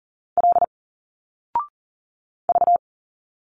Morse code, heard as a keyed tone, about 40 wpm, 700 Hz. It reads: L V
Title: Single Letters 40wpm